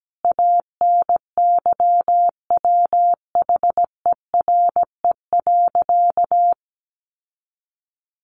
Morse code, heard as a keyed tone, about 17 wpm, 700 Hz. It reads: ANYWHERE.
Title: Sentences from Top 1500 Words 17wpm